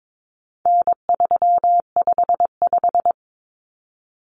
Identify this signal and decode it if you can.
Morse code, heard as a keyed tone, about 22 wpm, 700 Hz. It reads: N355
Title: Common Cut Numbers - 4-Digit 22wpm